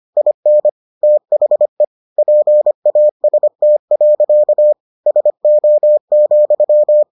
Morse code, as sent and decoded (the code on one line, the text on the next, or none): .. -. / - .... / .--. .- ... - .-.-.- / ... --- --..--
IN THE PAST. SO,